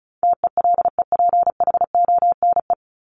Morse code, heard as a keyed tone, about 34 wpm, 700 Hz. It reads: TELEPHONE
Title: Top 1501-1750 Words 35wpm